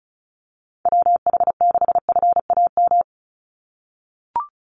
Morse code, heard as a keyed tone, about 34 wpm, 700 Hz. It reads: WH6FAM
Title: USA Call Signs 35wpm